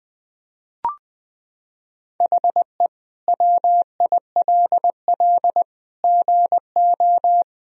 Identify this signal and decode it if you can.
Morse code, heard as a keyed tone, about 20 wpm, 700 Hz. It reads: HE WILL GO
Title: Sentences from Top 1000 Words - Review Part 2 - 20wpm